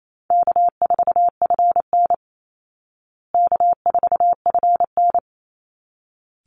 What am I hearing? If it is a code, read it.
Morse code, heard as a keyed tone, about 28 wpm, 700 Hz. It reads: K4FN K4FN
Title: Easy Call Signs Repeated 2x - 28wpm